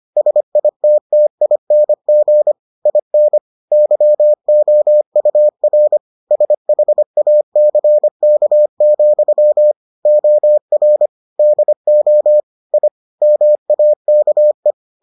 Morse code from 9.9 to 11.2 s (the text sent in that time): OR